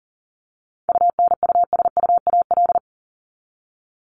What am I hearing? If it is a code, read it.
Morse code, heard as a keyed tone, about 40 wpm, 700 Hz. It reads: UNUSUAL
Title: Top 1251-1500 Words 40wpm